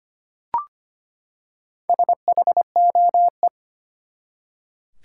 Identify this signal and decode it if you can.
Morse code, heard as a keyed tone, about 25 wpm, 700 Hz. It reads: SHOE
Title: Four Letter Words - 25wpm